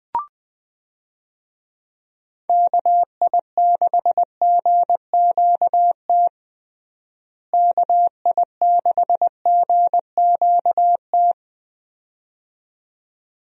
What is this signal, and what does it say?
Morse code, heard as a keyed tone, about 20 wpm, 700 Hz. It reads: KI6GQT KI6GQT
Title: USA Call Signs - 6 Characters - Repeated 2x - 20wpm